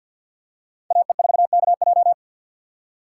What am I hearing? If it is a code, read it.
Morse code, heard as a keyed tone, about 49 wpm, 690 Hz. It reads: AE4KJ
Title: USA Call Signs - 5 Characters - 2x Character Spacing - 50wpm